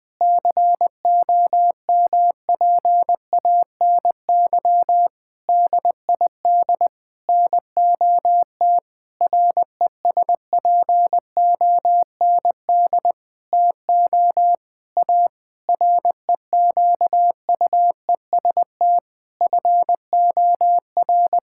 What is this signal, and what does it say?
Morse code, heard as a keyed tone, about 20 wpm, 700 Hz. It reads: COMPANY DID NOT RESPOND TO A REQUEST FOR
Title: Sentences with 8-Word Phrases - 20wpm